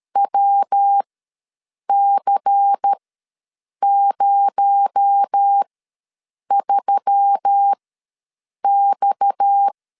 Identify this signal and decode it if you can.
Morse code, heard as a keyed tone, about 13 wpm, spaced as for 7 wpm, 800 Hz. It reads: WC03X
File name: LEZIONI / 6 PKX0 / 6 PKX0 ws 12wpm 3448